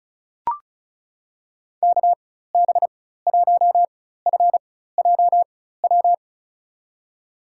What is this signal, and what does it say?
Morse code, heard as a keyed tone, about 36 wpm, spaced as for 16 wpm, 700 Hz. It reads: KB1FJW
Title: USA Call Signs - 6 Characters - 4x Character Spacing - 35wpm